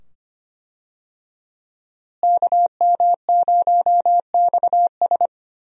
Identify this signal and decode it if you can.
Morse code, heard as a keyed tone, about 25 wpm, 700 Hz. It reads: KM0XS
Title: USA Call Signs - 5 Characters - Repeated 2x - 25wpm